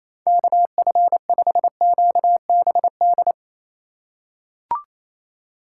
Morse code, heard as a keyed tone, about 28 wpm, 700 Hz. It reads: KF5QBD
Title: USA Call Signs - 6 Characters - 3x Character Spacing - 28wpm